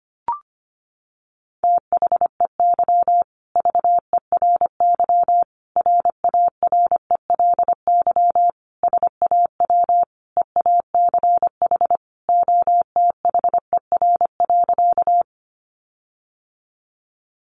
Morse code, heard as a keyed tone, about 25 wpm, 700 Hz. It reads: THEY VERY RARELY SAW EACH OTHER.
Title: Sentences from Top 1400 Words 25wpm